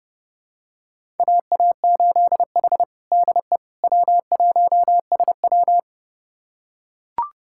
Morse code, heard as a keyed tone, about 30 wpm, 700 Hz. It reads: AA8H DE W1SW
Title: USA Callsign DE Callsign - 4 Characters - 30wpm